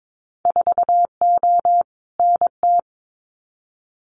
Morse code, heard as a keyed tone, about 22 wpm, 700 Hz. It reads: 4O NT
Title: Winter Field Day Exchanges Encore 22wpm